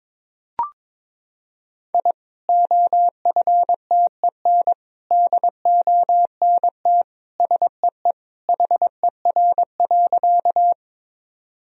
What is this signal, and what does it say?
Morse code, heard as a keyed tone, about 22 wpm, 700 Hz. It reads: I OFTEN DONT SEE HER.
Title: Sentences from Top 1000 Words - Review Part 1 - 22wpm